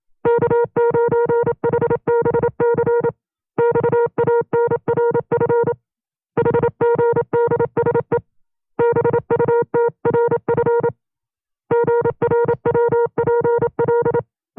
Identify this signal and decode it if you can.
Morse code, heard as a keyed tone, about 27 wpm, 470 Hz. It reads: K9HBC XANRF HGDSE BUTRF GRWPL
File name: FreeFiles / CW-practice / 25WPM-side1